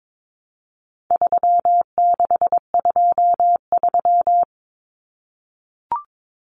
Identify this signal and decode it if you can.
Morse code, heard as a keyed tone, about 22 wpm, 700 Hz. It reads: 3623
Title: Numbers - 4-Digit 22wpm